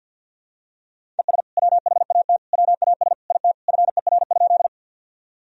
Morse code, heard as a keyed tone, about 52 wpm, 700 Hz. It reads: ES WHAT WAS IT FER?
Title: Sentences from Top 100 Words and Abbreviations 50wpm